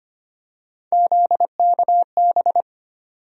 Morse code, heard as a keyed tone, about 25 wpm, 700 Hz. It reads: ZKB
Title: USA Call Sign Postfixes - 3 Letters 25wpm